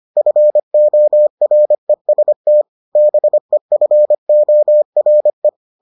Morse code, heard as a keyed tone, about 25 wpm, 600 Hz. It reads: FOREST BEFORE